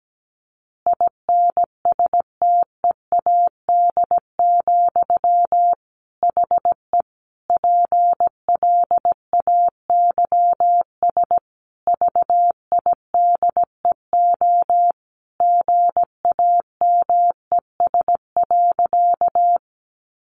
Morse code, heard as a keyed tone, about 17 wpm, 700 Hz. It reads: INSTEAD, HE PLAYS VIDEO GAMES.
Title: Additional Easy Phrases 17wpm